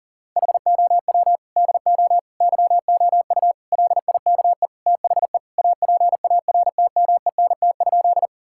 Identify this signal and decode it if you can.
Morse code, heard as a keyed tone, about 40 wpm, 700 Hz. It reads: HOW DO YOU LIKE THE APARTMENT?
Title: Sentences from Top 1000 Words - Review Part 6 - 40wpm